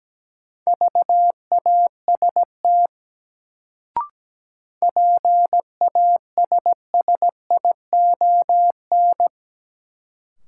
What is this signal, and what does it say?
Morse code, heard as a keyed tone, about 17 wpm, 700 Hz. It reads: VAST PASSION
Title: Sentences from Top 1300 Words - New Word Drill Alt - 17wpm